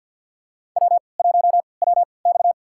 Morse code, heard as a keyed tone, about 52 wpm, spaced as for 29 wpm, 700 Hz. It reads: W1WX
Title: USA Call Signs - 4 Characters - 3x Character Spacing - 50wpm